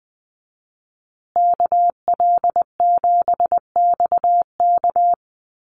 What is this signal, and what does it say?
Morse code, heard as a keyed tone, about 20 wpm, 700 Hz. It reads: KL7XK
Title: USA Call Signs Repeated 2x 20wpm